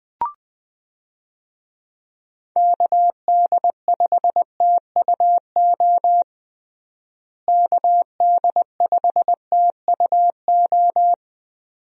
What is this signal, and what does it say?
Morse code, heard as a keyed tone, about 20 wpm, 700 Hz. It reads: KD5TUO KD5TUO
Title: USA Call Signs - 6 Characters - Repeated 2x - 20wpm